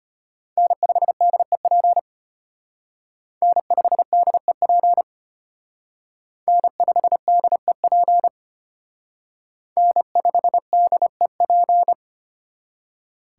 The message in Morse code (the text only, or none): N5DEP N5DEP N5DEP N5DEP